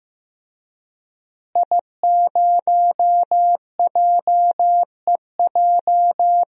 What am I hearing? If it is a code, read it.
Morse code, heard as a keyed tone, about 15 wpm, 700 Hz. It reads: I0JEJ
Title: DXCC-100 Call Signs - 5 Characters - Repeated 2x - 15wpm